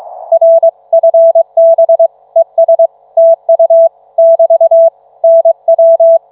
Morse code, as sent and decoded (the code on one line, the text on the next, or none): .-. ..-. -... / . ... / - ..- / -...- / -. .--
RFB ES TU <BT> NW